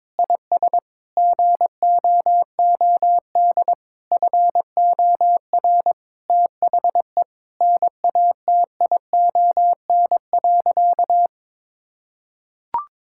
Morse code, heard as a keyed tone, about 22 wpm, 700 Hz. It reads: IS GOOD FOR THE NATION.
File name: practice / 22wpm / Sentences with 4-Word Phrases - 22wpm